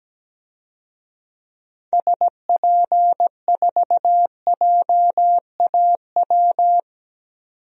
Morse code, heard as a keyed tone, about 17 wpm, 700 Hz. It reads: SP4JAW
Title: DXCC-100 Call Signs - 6 Characters - 17wpm